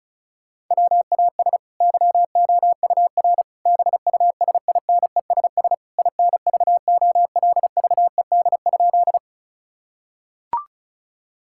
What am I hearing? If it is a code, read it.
Morse code, heard as a keyed tone, about 34 wpm, 700 Hz. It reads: WAS YOUR BUSINESS INVOLVED?
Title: Sentences from Top 1000 Words - Review Part 5 - 35wpm